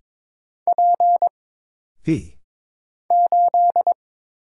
Morse code, heard as a keyed tone, about 22 wpm, 700 Hz. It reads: P 8
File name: Single Character (Letter:Number:Punctuation:Prosign) - Mind-Melt - 22wpm